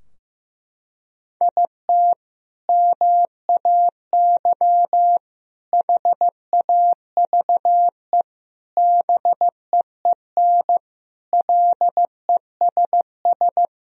Morse code, heard as a keyed tone, about 16 wpm, 700 Hz. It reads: IT MAY HAVE BEEN LESS
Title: Sentences from Top 1000 Words - Review Part 2 - 15wpm